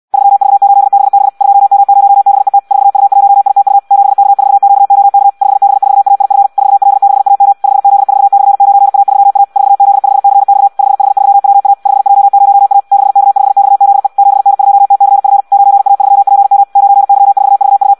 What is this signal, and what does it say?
Morse code, heard as a keyed tone, about 74 wpm, 800 Hz. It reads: 3V8BB 2M0BET 4U1ITU 7K4QOK 5B4AIF 4L6AM 5Z4/9A3A 4Z5CX 5H3WA 5J0A 6K5XZE 7N2TCF 9A3PM 9Y4VU